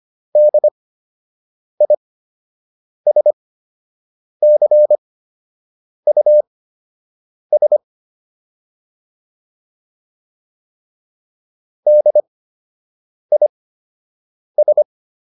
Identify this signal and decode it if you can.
Morse code, heard as a keyed tone, about 25 wpm, spaced as for 7 wpm, 600 Hz. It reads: DISCUS DIS